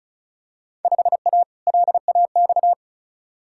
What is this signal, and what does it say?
Morse code, heard as a keyed tone, about 34 wpm, 700 Hz. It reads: 5A LAX
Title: Field Day Exchanges - 35wpm